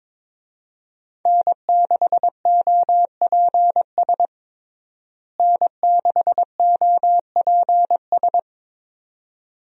Morse code, heard as a keyed tone, about 22 wpm, 700 Hz. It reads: N6OPS N6OPS
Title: USA Call Signs - 5 Characters - Repeated 2x - 22wpm